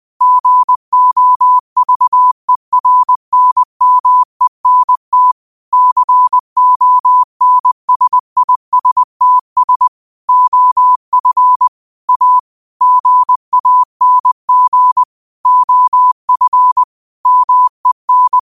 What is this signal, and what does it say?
Morse code, heard as a keyed tone, about 20 wpm, 1000 Hz. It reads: GOVERNMENT CONSISTS OF A GANG OF MEN
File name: qotdfiles / 20wpm